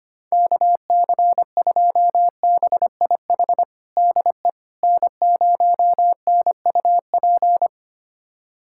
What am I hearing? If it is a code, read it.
Morse code, heard as a keyed tone, about 25 wpm, 700 Hz. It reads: KC2BIH DE N0NUP
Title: USA Callsign DE Callsign 25wpm